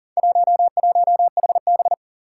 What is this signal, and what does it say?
Morse code, heard as a keyed tone, about 40 wpm, 700 Hz. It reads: 11HB
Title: Four Challenging Letter-Number Combinations 40wpm